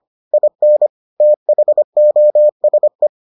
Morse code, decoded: IN THOSE

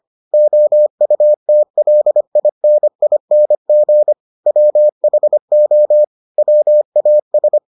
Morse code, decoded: OUTLINING WHO WAS

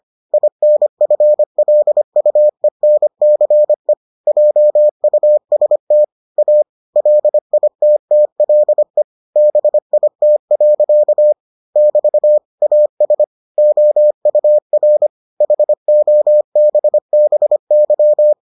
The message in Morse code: INFLUENCE JUST A LITTLE BIT. <BT> AS OUR HOBBY

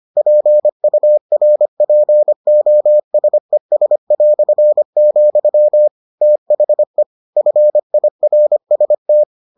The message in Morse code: PURPOSES", THE FIRST

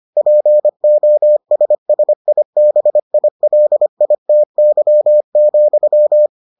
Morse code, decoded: POSSIBILITY,

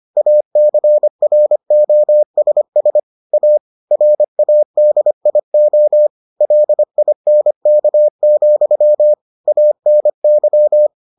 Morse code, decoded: ACROSS A RADIO LINK, ANY

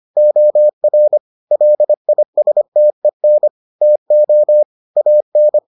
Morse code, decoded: OR LISTEN TO AN

